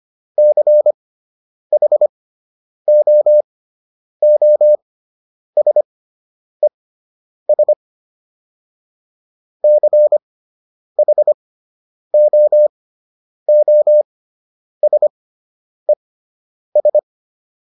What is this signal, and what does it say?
Morse code, heard as a keyed tone, about 25 wpm, spaced as for 9 wpm, 600 Hz. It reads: CHOOSES CHOOSES